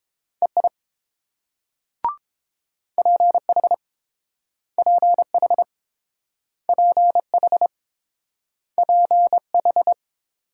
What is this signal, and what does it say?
Morse code, heard as a keyed tone, about 30 wpm, 700 Hz. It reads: EI PH PH PH PH